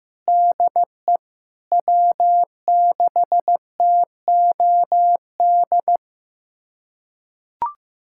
Morse code, heard as a keyed tone, about 16 wpm, 700 Hz. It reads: DE W6TOD
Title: USA Callsign DE Callsign - 5 Characters - 15wpm